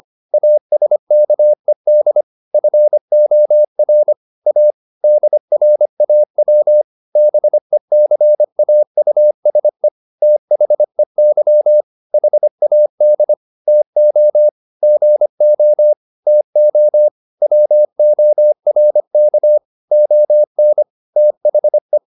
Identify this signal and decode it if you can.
Morse code, heard as a keyed tone, about 25 wpm, 600 Hz. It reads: ASKED FOR A DRAW BECAUSE THEY HAD TO GO TO WORK ON THE